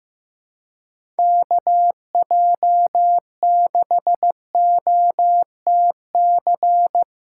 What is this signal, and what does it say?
Morse code, heard as a keyed tone, about 16 wpm, 700 Hz. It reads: KJ6OTC
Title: USA Callsign DE Callsign 15wpm